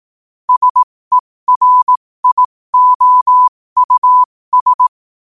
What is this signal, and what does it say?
Morse code, heard as a keyed tone, about 18 wpm, spaced as for 15 wpm, 1000 Hz. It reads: SERIOUS
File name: qotdfiles / 15wpm